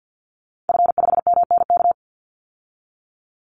Morse code, heard as a keyed tone, about 50 wpm, 700 Hz. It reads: F5MNK
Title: Call Signs - 50wpm